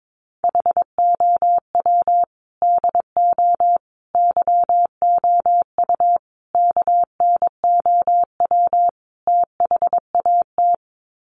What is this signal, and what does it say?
Morse code, heard as a keyed tone, about 22 wpm, 700 Hz. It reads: HOW DO YOU KNOW THAT
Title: Sets of 5 Words Encore 22wpm